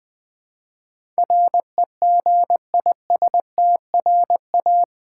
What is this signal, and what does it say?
Morse code, heard as a keyed tone, about 20 wpm, 700 Hz. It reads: REGISTRA
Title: Build A Word - 4 Letters At A Time - 12-Letter Words - 20wpm